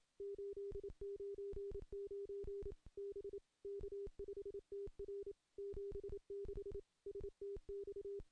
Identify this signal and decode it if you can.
Morse code, heard as a keyed tone, about 27 wpm, 400 Hz. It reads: 899 B K5TR 76 STX
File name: contests / CW-SS-k5tr- -1400